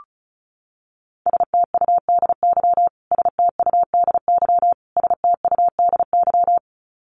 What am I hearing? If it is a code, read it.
Morse code, heard as a keyed tone, about 34 wpm, 700 Hz. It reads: STUDY STUDY STUDY
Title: Top 500 Words Repeated 35wpm